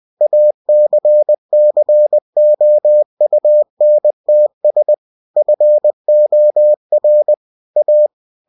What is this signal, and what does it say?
Morse code, heard as a keyed tone, about 20 wpm, 600 Hz. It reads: ACCOUNTS FOR A
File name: cw-books / winnie-the-pooh / winnie-20wpm-page-0027